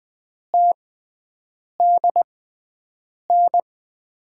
Morse code, heard as a keyed tone, about 20 wpm, spaced as for 7 wpm, 700 Hz. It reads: TDN
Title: USA Call Signs - 5 Characters - 6x Character Spacing - 20wpm